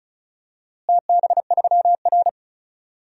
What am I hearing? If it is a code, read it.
Morse code, heard as a keyed tone, about 36 wpm, 700 Hz. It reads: TB3R
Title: DXCC-100 Call Signs - 4 Characters - Repeated 3x - 35wpm